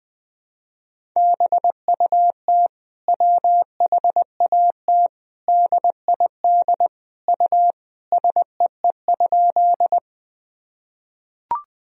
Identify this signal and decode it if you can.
Morse code, heard as a keyed tone, about 21 wpm, 700 Hz. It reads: BUT WHAT DID U SEE?
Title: Sentences from Top 100 Words and Abbreviations 20wpm